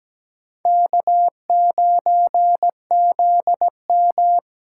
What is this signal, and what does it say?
Morse code, heard as a keyed tone, about 17 wpm, 700 Hz. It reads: K9ZM
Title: USA Call Signs - 4 Characters - 2x Character Spacing - 17wpm